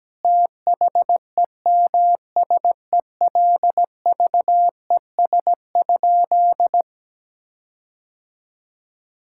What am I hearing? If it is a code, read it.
Morse code, heard as a keyed tone, about 17 wpm, 700 Hz. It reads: THEMSELVES?
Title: Sentences from Top 1600 Words 17wpm